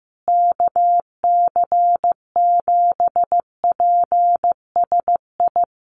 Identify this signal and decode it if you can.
Morse code, heard as a keyed tone, about 15 wpm, 700 Hz. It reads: KC7PSI
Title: USA Call Signs Repeated 3x 15wpm